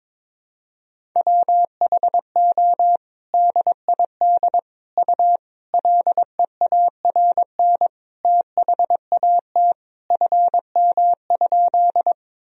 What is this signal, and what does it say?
Morse code, heard as a keyed tone, about 22 wpm, 700 Hz. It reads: WHO DID U LEARN THAT FM?
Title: Sentences from Top 300 Words and Abbreviations 22wpm